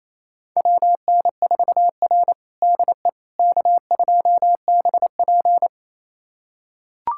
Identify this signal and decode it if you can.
Morse code, heard as a keyed tone, about 28 wpm, 700 Hz. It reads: WN4R DE K2BP
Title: USA Callsign DE Callsign - 4 Characters - 28wpm